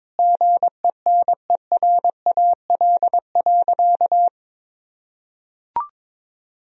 Morse code, heard as a keyed tone, about 22 wpm, 700 Hz. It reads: GENERAL.
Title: Sentences from Top 1000 Words - Review Part 3 - 22wpm